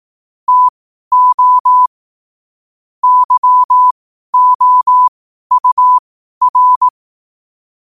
Morse code, heard as a keyed tone, about 18 wpm, spaced as for 12 wpm, 1000 Hz. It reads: TO YOUR
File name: 12wpm